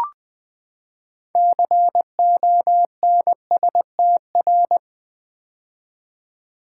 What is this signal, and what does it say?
Morse code, heard as a keyed tone, about 20 wpm, 700 Hz. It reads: CONSTR
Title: Build A Word - 6 Letters At A Time - 12-Letter Words - 20wpm